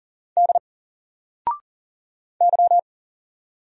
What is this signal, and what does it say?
Morse code, heard as a keyed tone, about 39 wpm, 700 Hz. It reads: D Y
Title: Single Letter-Number 40wpm